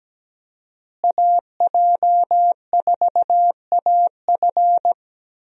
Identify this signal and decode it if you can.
Morse code, heard as a keyed tone, about 17 wpm, 700 Hz. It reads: AJ4AF
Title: USA Call Signs - 5 Characters - Repeated 3x - 17wpm